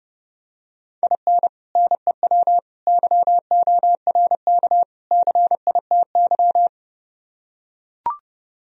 Morse code, heard as a keyed tone, about 30 wpm, 700 Hz. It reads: IN NEW YORK CITY